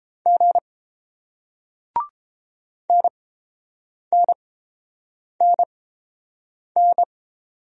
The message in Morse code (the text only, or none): G NNNN